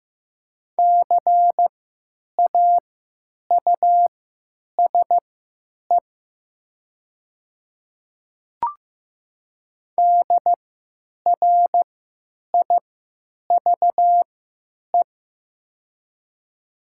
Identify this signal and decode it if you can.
Morse code, heard as a keyed tone, about 15 wpm, spaced as for 9 wpm, 700 Hz. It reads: CAUSE DRIVE
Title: 3x Character Spacing - Top 500 Words - 15wpm